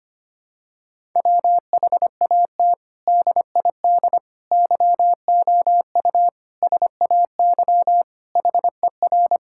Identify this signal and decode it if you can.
Morse code, heard as a keyed tone, about 25 wpm, 700 Hz. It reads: WHAT DID YOU SAY HER
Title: Sentences from Top 1000 Words - Review Part 6 - 25wpm